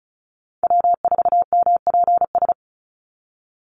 Morse code, heard as a keyed tone, about 34 wpm, 700 Hz. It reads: W4MPS
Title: NA SOTA Chaser Callsigns 35wpm